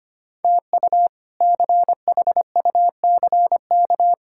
Morse code, heard as a keyed tone, about 25 wpm, 700 Hz. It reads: TU CHUCK
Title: K1USN SST QSO Practice - 25wpm